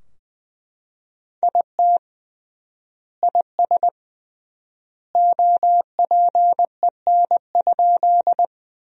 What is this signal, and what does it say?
Morse code, heard as a keyed tone, about 21 wpm, 700 Hz. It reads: IT IS OPEN?
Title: 3x Word-Spacing - Sentences from Top 300 Words - 20wpm